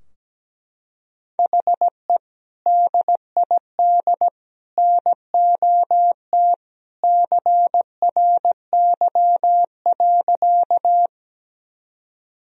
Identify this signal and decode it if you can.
Morse code, heard as a keyed tone, about 17 wpm, 700 Hz. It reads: HE DID NOT CRY.
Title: Sentences from Top 1000 Words - Review Part 2 - 17wpm